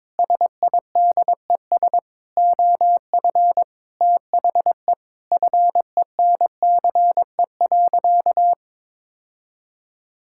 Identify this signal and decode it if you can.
Morse code, heard as a keyed tone, about 22 wpm, 700 Hz. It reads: SIDES OF THE FENCE.
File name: practice / 22wpm / Sentences with 4-Word Phrases - 22wpm